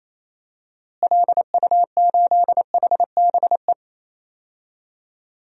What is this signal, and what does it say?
Morse code, heard as a keyed tone, about 27 wpm, 700 Hz. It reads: LU8HBE
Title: DXCC-100 Call Signs - 6 Characters - Repeated 3x - 28wpm